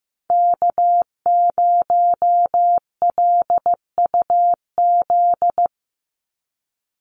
Morse code, heard as a keyed tone, about 15 wpm, 700 Hz. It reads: K0LUZ
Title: Call Signs Repeated 2x - 15wpm